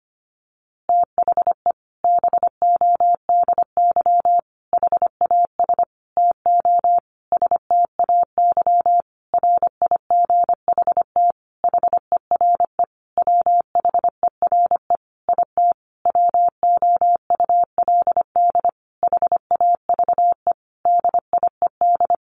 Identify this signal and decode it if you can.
Morse code, heard as a keyed tone, about 25 wpm, 700 Hz. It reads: THE BODY HAS TO STAY RIGHT HERE WHERE IT WOULD HAVE DIED
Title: A Sound Of Thunder 25wpm